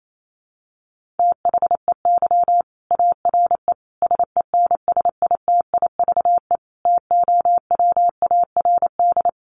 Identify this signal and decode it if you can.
Morse code, heard as a keyed tone, about 28 wpm, 700 Hz. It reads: THEY ARE SENSITIVE TOWARD
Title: Sentences from Top 1400 Words 28wpm